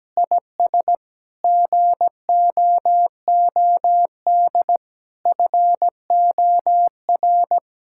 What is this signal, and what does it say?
Morse code, heard as a keyed tone, about 17 wpm, 700 Hz. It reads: IS GOOD FOR
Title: Sentences from Top 1000 Words - Review Part 6 - 17wpm